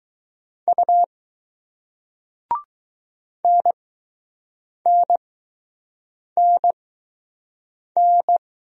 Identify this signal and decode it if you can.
Morse code, heard as a keyed tone, about 20 wpm, spaced as for 6 wpm, 700 Hz. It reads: U NNNN